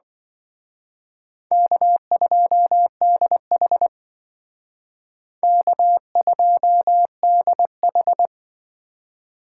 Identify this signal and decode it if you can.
Morse code, heard as a keyed tone, about 23 wpm, 700 Hz. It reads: K2DH K2DH
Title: Speed Racing - USA Call Signs 20wpm